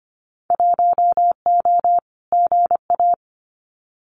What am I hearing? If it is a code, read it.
Morse code, heard as a keyed tone, about 25 wpm, 700 Hz. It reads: 1O GA
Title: Winter Field Day Exchanges 25wpm